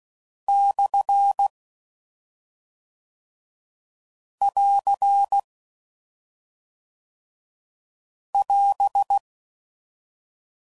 Morse code, heard as a keyed tone, about 16 wpm, 800 Hz. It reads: / <AR> <AS>